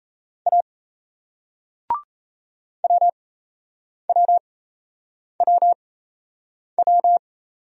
Morse code, heard as a keyed tone, about 37 wpm, spaced as for 8 wpm, 700 Hz. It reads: A WWWW